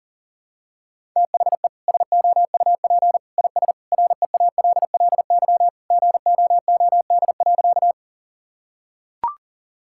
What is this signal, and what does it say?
Morse code, heard as a keyed tone, about 40 wpm, 700 Hz. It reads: THE SOUP IS REALLY GOOD.
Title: Sentences from Top 1000 Words - Review Part 6 - 40wpm